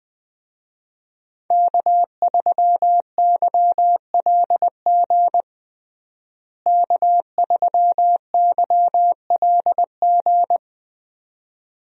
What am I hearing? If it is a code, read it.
Morse code, heard as a keyed tone, about 21 wpm, 700 Hz. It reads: K3YLG K3YLG
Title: USA Call Signs - 5 Characters - Repeated 2x - 20wpm